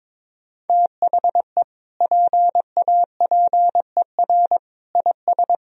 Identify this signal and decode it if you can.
Morse code, heard as a keyed tone, about 21 wpm, 700 Hz. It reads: THE PAPER IS
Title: Sentences from Top 1000 Words - Review Part 1 - 22wpm